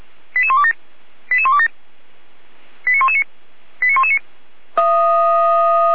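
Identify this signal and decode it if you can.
Five-tone selective call (ZVEI), heard as quick runs of five tones, tones of about 70 ms. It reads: 88127 88127 89119 89119
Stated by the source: Five-tone ZVEI selective call 88129 received on SDR.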